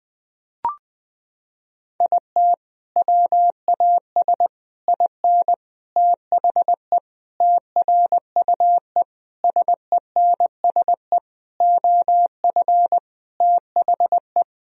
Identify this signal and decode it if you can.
Morse code, heard as a keyed tone, about 20 wpm, 700 Hz. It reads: IT WAS IN THE TRUE SENSE OF THE
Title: Sentences from Top 1000 Words - Review Part 4 - 20wpm